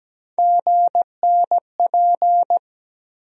Morse code, heard as a keyed tone, about 17 wpm, 700 Hz. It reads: GNP
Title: USA Call Sign Postfixes - 3 Letters 17wpm